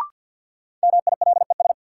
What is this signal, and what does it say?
Morse code, heard as a keyed tone, about 52 wpm, 700 Hz. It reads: MILES
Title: Sentences from Top 1000 Words - Review Part 3 - New Word Drill - 50wpm